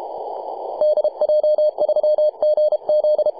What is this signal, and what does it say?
Morse code, heard as a keyed tone, about 32 wpm, 590 Hz, with radio noise. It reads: DJ3GZ
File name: e51rat-14-039mhz-cw